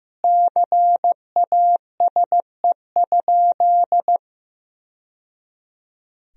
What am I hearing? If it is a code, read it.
Morse code, heard as a keyed tone, about 15 wpm, 700 Hz. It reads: CASE?
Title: Sentences from Top 1000 Words - Review Part 3 - 15wpm